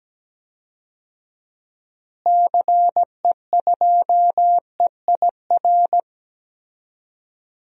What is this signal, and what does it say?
Morse code, heard as a keyed tone, about 17 wpm, 700 Hz. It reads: CE2EIR
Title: DXCC-100 Call Signs - 6 Characters - Repeated 3x - 17wpm